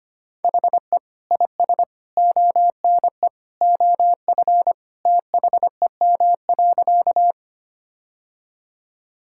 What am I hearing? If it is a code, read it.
Morse code, heard as a keyed tone, about 25 wpm, 700 Hz. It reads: HE IS ONE OF THEM.
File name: practice / 25wpm / Sentences with 3-Word Phrases - 25wpm